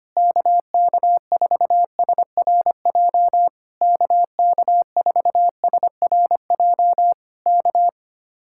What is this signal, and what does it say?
Morse code, heard as a keyed tone, about 24 wpm, 700 Hz. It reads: KK4SRJ KK4SRJ K
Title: Speed-Racing Common QSO Phrases - 25wpm